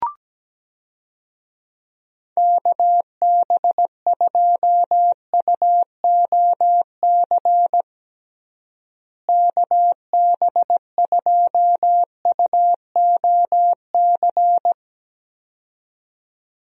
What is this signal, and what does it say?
Morse code, heard as a keyed tone, about 17 wpm, 700 Hz. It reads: KB2UOC KB2UOC
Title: USA Call Signs - 6 Characters - Repeated 2x - 17wpm